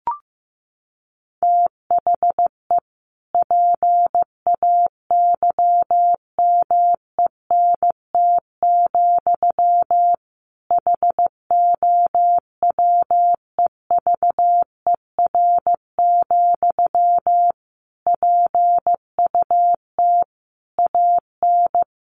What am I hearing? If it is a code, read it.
Morse code, heard as a keyed tone, about 15 wpm, 700 Hz. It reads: THE PAYMENT, HOWEVER, PUT AN
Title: Sentences from Top 1200 Words 15wpm